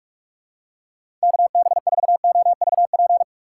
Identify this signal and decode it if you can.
Morse code, heard as a keyed tone, about 47 wpm, 700 Hz. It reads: KB4OVP
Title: USA Call Signs - 6 Characters - 3x Character Spacing - 45wpm